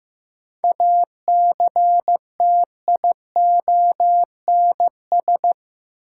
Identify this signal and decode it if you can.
Morse code, heard as a keyed tone, about 15 wpm, 700 Hz. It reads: ACTIONS
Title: Sentences from Top 1000 Words - Review Part 6 - New Word Drill - 15wpm